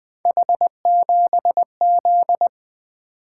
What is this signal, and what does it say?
Morse code, heard as a keyed tone, about 20 wpm, 700 Hz. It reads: H7Z
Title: Three Letter-Number Combinations - 20wpm